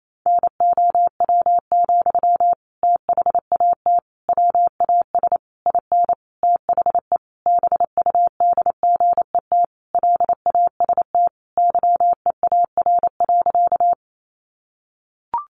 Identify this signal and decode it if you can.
Morse code, heard as a keyed tone, about 28 wpm, 700 Hz. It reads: NOW, THAT WAS IN THE BUDGET LAST YEAR.
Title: Sentences from Top 1200 Words 28wpm